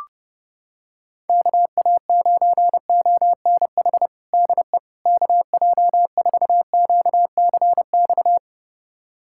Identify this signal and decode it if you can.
Morse code, heard as a keyed tone, about 31 wpm, 700 Hz. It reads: KA9ONH DE KJ4QCX
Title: USA Callsign DE Callsign - 6 Characters - 30wpm